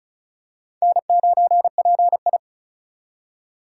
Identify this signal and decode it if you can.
Morse code, heard as a keyed tone, about 34 wpm, 700 Hz. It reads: N9PI
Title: USA Call Signs - 4 Characters - 4x Character Spacing - 35wpm